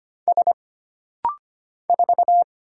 Morse code, heard as a keyed tone, about 25 wpm, 700 Hz. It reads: S 4
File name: Single Character (Letter:Number:Punctuation:Prosign) - Rapid-Fire - 25wpm